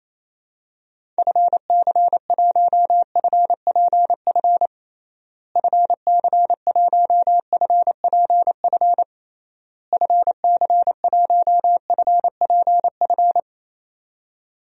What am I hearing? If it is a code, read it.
Morse code, heard as a keyed tone, about 27 wpm, 700 Hz. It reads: FC1FPF FC1FPF FC1FPF
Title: DXCC-100 Call Signs - 6 Characters - Repeated 3x - 28wpm